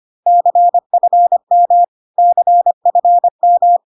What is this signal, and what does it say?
Morse code, heard as a keyed tone, about 25 wpm, 700 Hz. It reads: CFM CFM
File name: CW abbreviations alphabetical repeated 3 times 25 25wpm